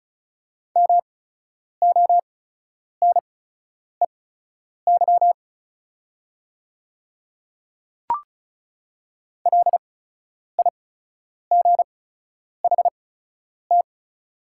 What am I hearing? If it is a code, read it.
Morse code, heard as a keyed tone, about 36 wpm, spaced as for 10 wpm, 700 Hz. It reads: MONEY LIGHT
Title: 8x Character Spacing - Top 500 Words - 35wpm